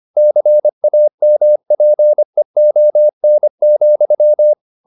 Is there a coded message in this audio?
Yes: Morse code from 0.2 to 4.5 s, heard as a keyed tone, about 25 wpm, 600 Hz. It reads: CAMPEON,